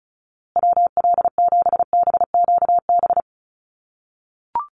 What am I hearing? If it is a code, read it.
Morse code, heard as a keyed tone, about 34 wpm, 700 Hz. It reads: WL7BQB
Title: USA Call Signs Repeated 3x 35wpm